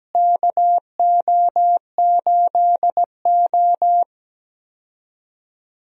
Morse code, heard as a keyed tone, about 17 wpm, 700 Hz. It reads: KO8O
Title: USA Call Signs - 4 Characters - Repeated 2x - 17wpm